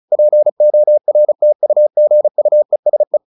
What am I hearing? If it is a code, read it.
Morse code, heard as a keyed tone, about 34 wpm, 600 Hz. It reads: PORTUGUESE
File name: the old man and the sea / the old man and the sea 1 35wpm